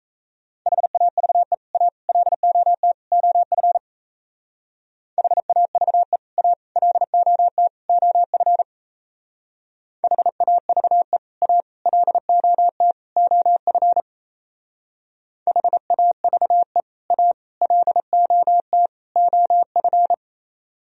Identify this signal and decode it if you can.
Morse code, heard as a keyed tone, about 35 wpm, 700 Hz. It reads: HAVE A LOT OF HAVE A LOT OF HAVE A LOT OF HAVE A LOT OF
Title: Speed-Racing - Sets of 4 Words 28wpm